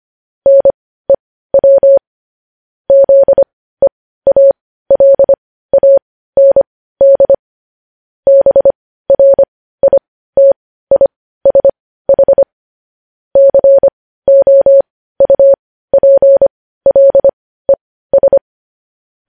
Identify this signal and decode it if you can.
Morse code, heard as a keyed tone, about 25 wpm, spaced as for 15 wpm, 550 Hz. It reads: NEW ZEALAND BRITISH COUPLES
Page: news headlines in morse code at 15 wpm